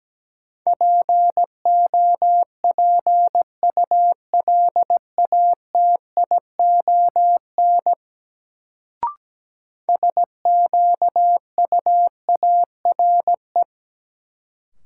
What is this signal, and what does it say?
Morse code, heard as a keyed tone, about 17 wpm, 700 Hz. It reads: POPULATION SQUARE
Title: Sentences from Top 1000 Words - Review Part 7 - New Word Drill - 17wpm